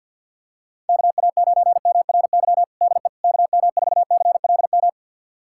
Morse code, heard as a keyed tone, about 48 wpm, 700 Hz. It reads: KA9MRY DE KM4CLM
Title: USA Callsign DE Callsign - 6 Characters - 50wpm